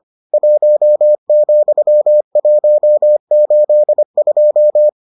Morse code, heard as a keyed tone, about 25 wpm, 600 Hz. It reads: 1,182